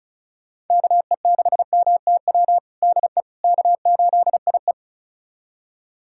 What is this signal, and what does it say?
Morse code, heard as a keyed tone, about 36 wpm, 700 Hz. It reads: KE6MTW DE K8IE
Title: USA Callsign DE Callsign 35wpm